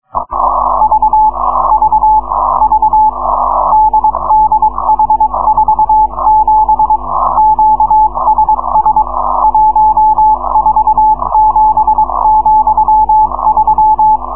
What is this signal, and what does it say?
Morse code, heard as a keyed tone, about 22 wpm, 860 Hz. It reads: V V V DDH47 QSS 147,3